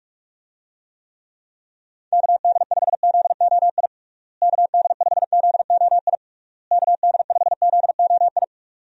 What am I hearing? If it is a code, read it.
Morse code, heard as a keyed tone, about 41 wpm, 700 Hz. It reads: KD5ZOI KD5ZOI KD5ZOI
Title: USA Call Signs - 6 Characters - Repeated 3x - 45wpm